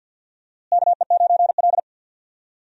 Morse code, heard as a keyed tone, about 51 wpm, 700 Hz. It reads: KE9L
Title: USA Call Signs - 4 Characters - Repeated 3x - 50wpm